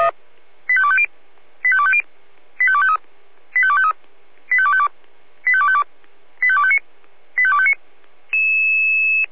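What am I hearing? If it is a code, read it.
Five-tone selective call (ZVEI), heard as quick runs of five tones, tones of about 70 ms. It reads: 85250 85250 85253 85253 85252 85252 85259 85259
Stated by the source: ZVEI five-tone call 85217, received by SDR.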